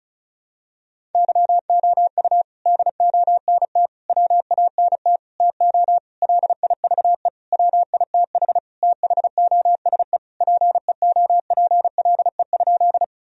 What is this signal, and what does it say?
Morse code, heard as a keyed tone, about 36 wpm, 700 Hz. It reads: YOU DONT WANT TO LIVE WITH THOSE PEOPLE?
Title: Sentences from Top 1000 Words - Review Part 1 - 35wpm